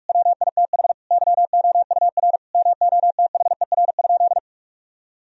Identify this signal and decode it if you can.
Morse code, heard as a keyed tone, about 46 wpm, 700 Hz. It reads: WITH YOUR MOTHER?
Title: Sentences from Top 1000 Words - Review Part 3 - 45wpm